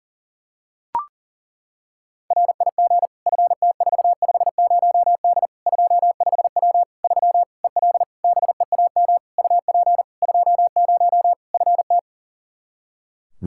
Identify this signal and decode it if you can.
Morse code, heard as a keyed tone, about 37 wpm, 700 Hz. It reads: RIG FT450D 25W 3 EL BEAM UP 20 FT
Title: Practice QSO - Exchange 2 - Info - 40wpm